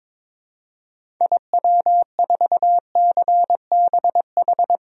Morse code, heard as a keyed tone, about 22 wpm, 700 Hz. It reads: IW4CBH
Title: DXCC-100 Call Signs - 6 Characters - 22wpm